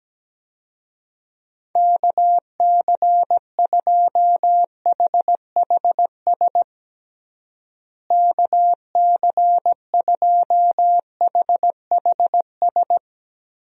Morse code, heard as a keyed tone, about 17 wpm, 700 Hz. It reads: KC2HHS KC2HHS
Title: USA Call Signs - 6 Characters - Repeated 2x - 17wpm